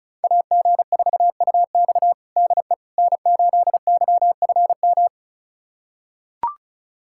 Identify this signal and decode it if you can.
Morse code, heard as a keyed tone, about 34 wpm, 700 Hz. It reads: AG4UX DE N8YFM
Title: USA Callsign DE Callsign - 5 Characters - 35wpm